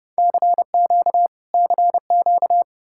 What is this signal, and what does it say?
Morse code, heard as a keyed tone, about 35 wpm, 700 Hz. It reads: CQ CQ